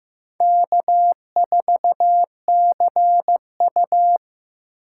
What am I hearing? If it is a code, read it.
Morse code, heard as a keyed tone, about 15 wpm, 700 Hz. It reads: K4CU
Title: USA Call Signs - 4 Characters - 6x Character Spacing - 15wpm